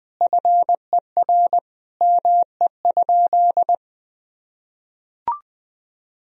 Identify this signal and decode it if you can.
Morse code, heard as a keyed tone, about 20 wpm, 700 Hz. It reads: FER ME?
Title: Sentences from Top 200 Words and Abbreviations 20wpm